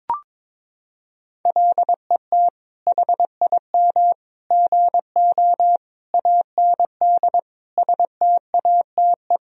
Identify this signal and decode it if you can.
Morse code, heard as a keyed tone, about 22 wpm, 700 Hz. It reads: LET HIM GO AND STATE
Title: Sentences from Top 1000 Words - Review Part 2 - 22wpm